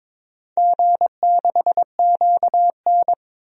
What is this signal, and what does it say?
Morse code, heard as a keyed tone, about 22 wpm, 700 Hz. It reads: G6QN
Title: DXCC-100 Call Signs - 4 Characters - 22wpm